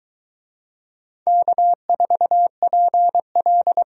Morse code, heard as a keyed tone, about 23 wpm, 700 Hz. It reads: K4PL